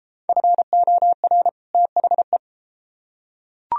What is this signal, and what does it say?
Morse code, heard as a keyed tone, about 30 wpm, 700 Hz. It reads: FOR THE